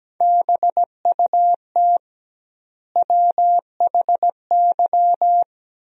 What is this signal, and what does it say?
Morse code, heard as a keyed tone, about 17 wpm, 700 Hz. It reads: BUT WHY
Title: 2x Word-Spacing - Sentences from Top 200 Words - 17wpm